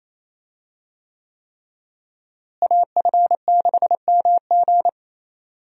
Morse code, heard as a keyed tone, about 28 wpm, 700 Hz. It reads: AF6MG
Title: USA Call Signs - 5 Characters - Repeated 3x - 28wpm